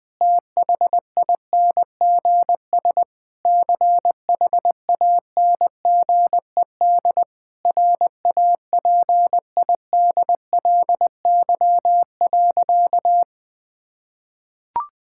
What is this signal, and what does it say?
Morse code, heard as a keyed tone, about 20 wpm, 700 Hz. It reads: THINGS CHANGED RAPIDLY.
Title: Sentences with 5-Word Phrases - 20wpm